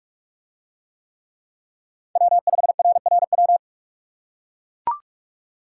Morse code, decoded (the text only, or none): W5RRW